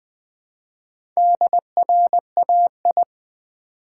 Morse code, heard as a keyed tone, about 20 wpm, 700 Hz. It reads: DRAI
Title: Build A Word - 1 Letter At A Time - 5-Letter Words - 20wpm